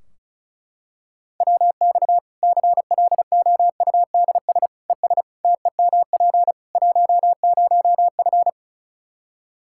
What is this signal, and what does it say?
Morse code, heard as a keyed tone, about 36 wpm, 700 Hz. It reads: WX CLOUDS ES TEMP 10F
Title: Practice QSO - Exchange 2 - Info - 35wpm